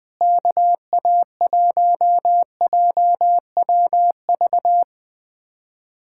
Morse code, heard as a keyed tone, about 19 wpm, 700 Hz. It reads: KA1JWV
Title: Speed-Racing Common QSO Phrases - 20wpm